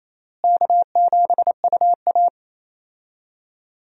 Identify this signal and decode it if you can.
Morse code, heard as a keyed tone, about 27 wpm, 700 Hz. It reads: K7UA
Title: USA Call Signs - 4 Characters - Repeated 3x - 28wpm